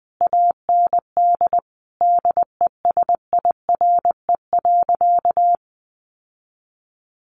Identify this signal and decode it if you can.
Morse code, heard as a keyed tone, about 19 wpm, 700 Hz. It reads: AND DESIRE.
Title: Sentences from Top 1100 Words 20wpm